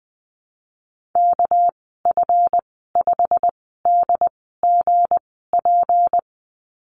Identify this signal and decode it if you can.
Morse code, heard as a keyed tone, about 20 wpm, spaced as for 14 wpm, 700 Hz. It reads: KF5DGP
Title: 238 Lesson 38g - Letter q - Call Signs 20wpm